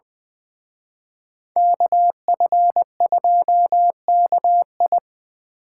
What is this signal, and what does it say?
Morse code, heard as a keyed tone, about 24 wpm, 700 Hz. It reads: KF2KI
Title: Speed Racing - USA 2x2 Call Signs 20wpm